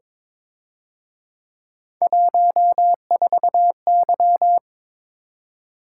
Morse code, heard as a keyed tone, about 26 wpm, 700 Hz. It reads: 14Y
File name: Speed-Racing - Three Letter-Number Combinations - 22wpm